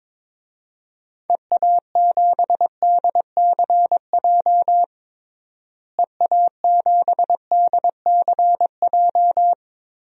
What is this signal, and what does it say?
Morse code, heard as a keyed tone, about 22 wpm, 700 Hz. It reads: EA7DCJ EA7DCJ
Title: DXCC-100 Call Signs - 6 Characters - Repeated 2x - 22wpm